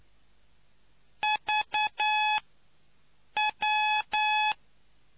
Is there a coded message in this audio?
Yes: Morse code from 1.2 to 4.5 s, heard as a keyed tone, about 9 wpm, spaced as for 6 wpm, 850 Hz. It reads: VW